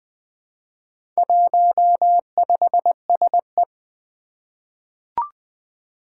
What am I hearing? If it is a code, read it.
Morse code, heard as a keyed tone, about 20 wpm, 700 Hz. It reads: 15SE
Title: Four Challenging Letter-Number Combinations 20wpm